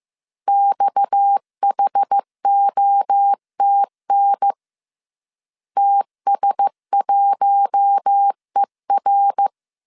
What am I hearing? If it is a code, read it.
Morse code, heard as a keyed tone, about 15 wpm, 800 Hz. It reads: XHOTN TS1ER
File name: CW / DET / 1 6 1500C S C 14wpm 2951